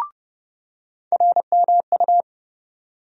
Morse code, heard as a keyed tone, about 30 wpm, 700 Hz. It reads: RMU
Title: USA Call Sign Postfixes - 3 Letters 30wpm